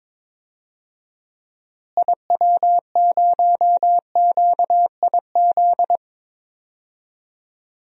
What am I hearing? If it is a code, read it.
Morse code, heard as a keyed tone, about 22 wpm, 700 Hz. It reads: IW0QIZ
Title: DXCC-100 Call Signs - 6 Characters - Repeated 3x - 22wpm